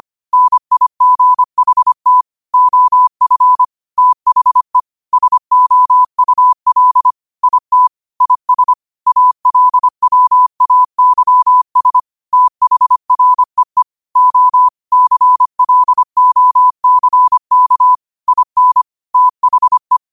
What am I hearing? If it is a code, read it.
Morse code, heard as a keyed tone, about 25 wpm, 1000 Hz. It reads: NIGHT OF THE SOUL IT IS ALWAYS THREE O CLOCK IN THE